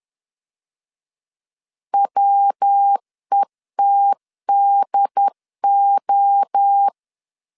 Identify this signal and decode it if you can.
Morse code, heard as a keyed tone, about 10 wpm, 800 Hz. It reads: WETDO